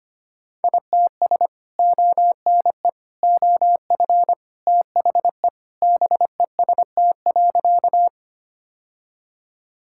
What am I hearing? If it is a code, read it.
Morse code, heard as a keyed tone, about 25 wpm, 700 Hz. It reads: ITS ONE OF THE BEST.
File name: practice / 25wpm / Sentences with 3-Word Phrases - 25wpm